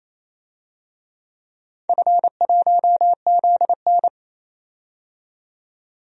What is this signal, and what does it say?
Morse code, heard as a keyed tone, about 28 wpm, 700 Hz. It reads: F1ZN
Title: DXCC-100 Call Signs - 4 Characters - 28wpm